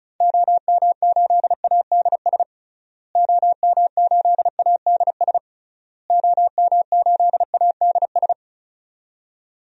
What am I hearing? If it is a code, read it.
Morse code, heard as a keyed tone, about 36 wpm, 700 Hz. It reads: OM8ADS OM8ADS OM8ADS
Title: DXCC-100 Call Signs - 6 Characters - Repeated 3x - 35wpm